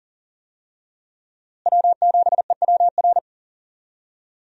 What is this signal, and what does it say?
Morse code, heard as a keyed tone, about 40 wpm, 700 Hz. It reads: W7EWR
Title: USA Call Signs - 5 Characters - Repeated 2x - 40wpm